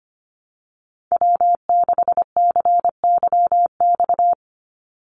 Morse code, heard as a keyed tone, about 25 wpm, 700 Hz. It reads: W6CYX
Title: Call Signs Repeated 3x - 25wpm